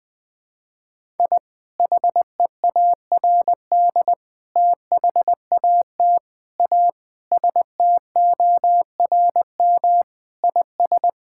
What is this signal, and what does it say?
Morse code, heard as a keyed tone, about 20 wpm, 700 Hz. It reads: I HEARD THAT A STORM IS
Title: Sentences with 2-Word Phrases - 20wpm